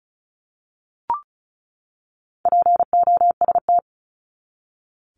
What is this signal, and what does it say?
Morse code, heard as a keyed tone, about 34 wpm, 700 Hz. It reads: POST
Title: Top 751-1000 Words 35wpm